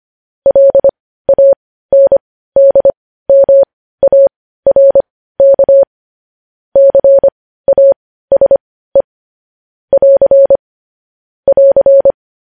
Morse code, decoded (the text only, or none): LANDMARK CASE <AR> <AR>